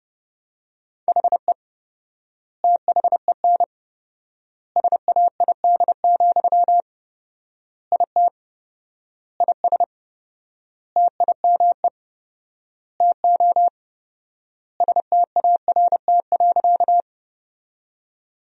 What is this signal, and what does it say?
Morse code, heard as a keyed tone, about 30 wpm, 700 Hz. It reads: HE THEN SAID, IT IS TIME TO START.
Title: 4x Word-Spacing - Sentences from Top 300 Words - 30wpm